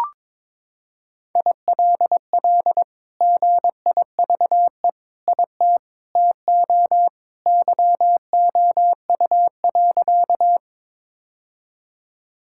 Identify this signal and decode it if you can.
Morse code, heard as a keyed tone, about 22 wpm, 700 Hz. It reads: ILL GIVE IT TO YOU.
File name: practice / 22wpm / Sentences with 2-Word Phrases - 22wpm